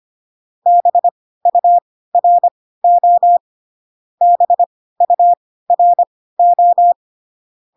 Morse code, heard as a keyed tone, about 25 wpm, spaced as for 16 wpm, 700 Hz. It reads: BURO BURO